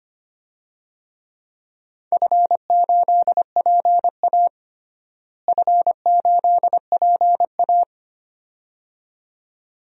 Morse code, heard as a keyed tone, about 25 wpm, 700 Hz. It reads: F8PA F8PA
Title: DXCC-100 Call Signs - 4 Characters - Repeated 2x - 25wpm